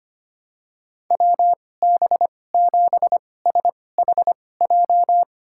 Morse code, heard as a keyed tone, about 25 wpm, spaced as for 18 wpm, 700 Hz. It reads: WB7SHJ
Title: USA Call Signs - 6 Characters - 2x Character Spacing - 25wpm